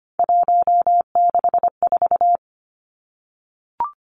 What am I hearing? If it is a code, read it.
Morse code, heard as a keyed tone, about 25 wpm, 700 Hz. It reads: 164
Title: Numbers - 3-Digit 25wpm